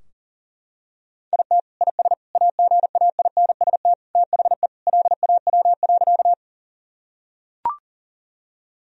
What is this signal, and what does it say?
Morse code, heard as a keyed tone, about 37 wpm, 700 Hz. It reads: IT IS AGAINST THE LAW.
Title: Sentences from Top 1000 Words - Review Part 3 - 40wpm